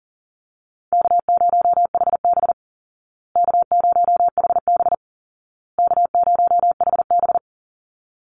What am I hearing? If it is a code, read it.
Morse code, heard as a keyed tone, about 40 wpm, 700 Hz. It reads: K0HB K0HB K0HB
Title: Easy Call Signs Repeated 3x - 40wpm